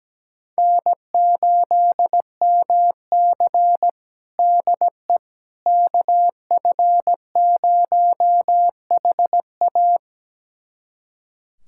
Morse code, heard as a keyed tone, about 17 wpm, 700 Hz. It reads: N8MC DE KF0HA
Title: USA Callsign DE Callsign 17wpm